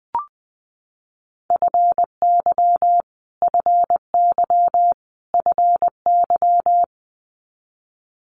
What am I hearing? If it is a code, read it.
Morse code, heard as a keyed tone, about 19 wpm, 700 Hz. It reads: FY FY FY
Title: Multi-letter Phonemes Repeated 3x 20wpm